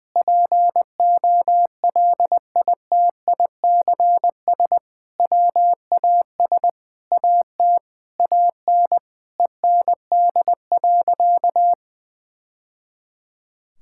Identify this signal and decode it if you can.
Morse code, heard as a keyed tone, about 20 wpm, 700 Hz. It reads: POLITICS WAS AT AN END.
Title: Sentences from Top 1000 Words - Review Part 6 - 20wpm